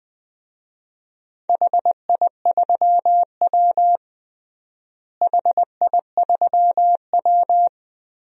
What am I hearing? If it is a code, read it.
Morse code, heard as a keyed tone, about 21 wpm, 700 Hz. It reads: HI3W HI3W
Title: DXCC-100 Call Signs - 4 Characters - Repeated 2x - 20wpm